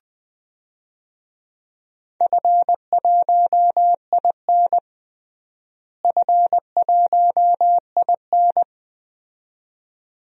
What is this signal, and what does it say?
Morse code, heard as a keyed tone, about 20 wpm, 700 Hz. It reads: F1IN F1IN
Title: DXCC-100 Call Signs - 4 Characters - Repeated 2x - 20wpm